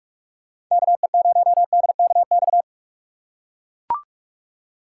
Morse code, heard as a keyed tone, about 44 wpm, 690 Hz. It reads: KE0DKX
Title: USA Call Signs - 4x Character Spacing - 45wpm